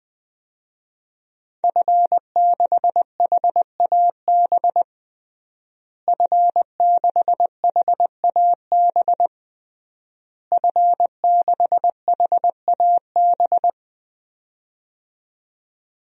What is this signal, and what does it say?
Morse code, heard as a keyed tone, about 20 wpm, 700 Hz. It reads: F6HAB F6HAB F6HAB
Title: DXCC-100 Call Signs - 5 Characters - Repeated 3x - 20wpm